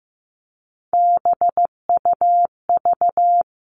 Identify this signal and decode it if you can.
Morse code, heard as a keyed tone, about 15 wpm, 700 Hz. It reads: BUV
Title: All Cut Numbers - 3-Digit 15wpm